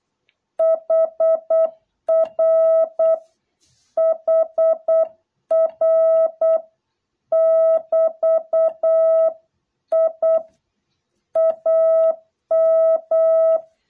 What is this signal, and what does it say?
Morse code, heard as a keyed tone, about 8 wpm, 660 Hz. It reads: HR HR <BT>I AM